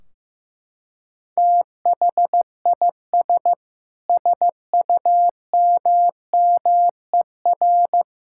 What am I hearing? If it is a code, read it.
Morse code, heard as a keyed tone, about 15 wpm, 700 Hz. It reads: THIS SUMMER
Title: Sentences from Top 1600 Words 15wpm